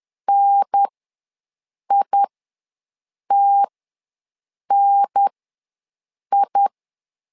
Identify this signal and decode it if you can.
Morse code, heard as a keyed tone, about 10 wpm, spaced as for 6 wpm, 800 Hz. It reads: NITNI
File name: LEZIONI / 2 UHDN / 2 UHDN ws 10wpm 2044